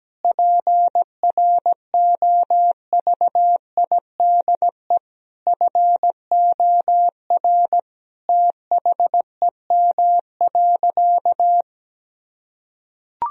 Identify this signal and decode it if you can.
Morse code, heard as a keyed tone, about 17 wpm, 700 Hz. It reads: PROVIDE FOR THEM.
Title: Sentences from Top 1000 Words - Review Part 3 - 17wpm